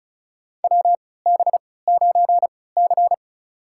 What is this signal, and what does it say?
Morse code, heard as a keyed tone, about 36 wpm, spaced as for 20 wpm, 690 Hz. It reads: WB9C
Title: USA Call Signs - 4 Characters - 3x Character Spacing - 35wpm